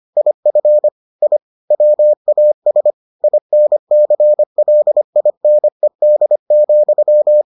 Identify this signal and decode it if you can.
Morse code, heard as a keyed tone, about 25 wpm, 600 Hz. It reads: IF I WAS INCLINED,